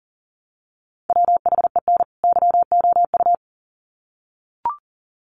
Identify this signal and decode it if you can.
Morse code, heard as a keyed tone, about 40 wpm, 700 Hz. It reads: WHEN YOU
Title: Sets of 2 Words 40wpm